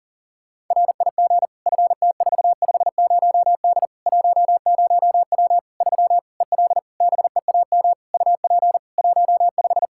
Morse code, heard as a keyed tone, about 37 wpm, 700 Hz. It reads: RIG FT450D 10W 3 EL BEAM UP 15
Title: Practice QSO - Exchange 2 - Info - 40wpm